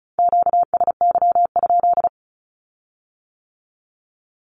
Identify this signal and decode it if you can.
Morse code, heard as a keyed tone, about 36 wpm, 700 Hz. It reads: QSY?
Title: Q-Codes with Spoken Meaning - 35wpm